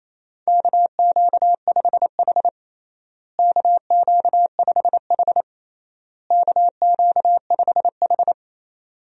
Morse code, heard as a keyed tone, about 28 wpm, 700 Hz. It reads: KQ5H KQ5H KQ5H
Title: USA Call Signs - 4 Characters - Repeated 3x - 28wpm